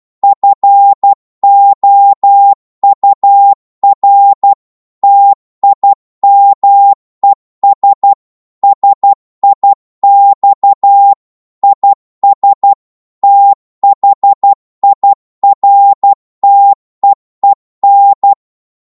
Morse code, heard as a keyed tone, about 12 wpm, 800 Hz. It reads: FOUR TIMES SIX IS THIRTEEN